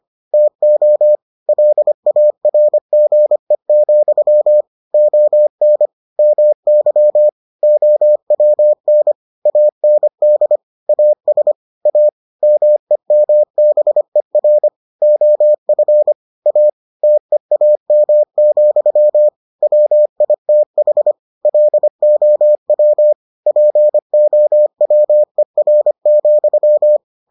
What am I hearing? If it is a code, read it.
Morse code, heard as a keyed tone, about 25 wpm, 600 Hz. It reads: TO LARGE, ON MY OWN AND AS A MEMBER OF A TEAM, WITH LOW POWER,